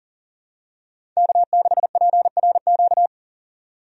Morse code, heard as a keyed tone, about 40 wpm, 700 Hz. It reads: K6PRQ
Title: USA Call Signs - 5 Characters - 40wpm